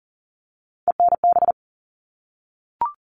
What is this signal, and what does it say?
Morse code, heard as a keyed tone, about 40 wpm, 700 Hz. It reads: ENB
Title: All Cut Numbers - 3-Digit 40wpm